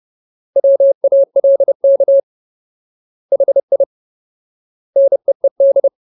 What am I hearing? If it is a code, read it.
Morse code, heard as a keyed tone, about 30 wpm, 550 Hz. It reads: WALK HI NEED
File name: Ultimate Word List / 30WPM Ultimate